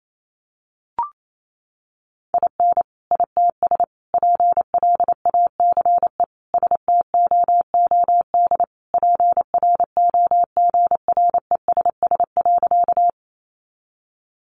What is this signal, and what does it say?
Morse code, heard as a keyed tone, about 28 wpm, 700 Hz. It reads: IN ITS PLACE STOOD PROGRESS.
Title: Sentences from Top 1400 Words 28wpm